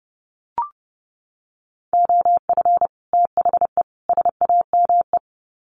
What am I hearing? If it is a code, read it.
Morse code, heard as a keyed tone, about 30 wpm, 700 Hz. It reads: OF THE SAME
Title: Sets of 3 Words 30wpm